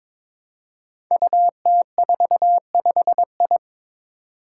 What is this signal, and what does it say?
Morse code, heard as a keyed tone, about 22 wpm, 700 Hz. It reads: UT45I
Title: Five Challenging Letter-Number Combinations 22wpm